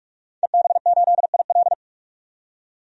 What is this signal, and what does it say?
Morse code, heard as a keyed tone, about 46 wpm, 700 Hz. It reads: EB8IL
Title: DXCC-100 Call Signs - 5 Characters - 45wpm